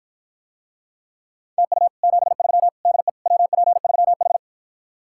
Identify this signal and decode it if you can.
Morse code, heard as a keyed tone, about 54 wpm, 700 Hz. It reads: TU 73 DE WP3H